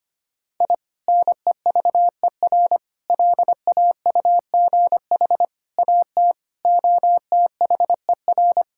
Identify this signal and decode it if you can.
Morse code, heard as a keyed tone, about 25 wpm, 700 Hz. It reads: I NEVER LAUGH AT OTHER
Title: Sentences from Top 1000 Words - Review Part 6 - 25wpm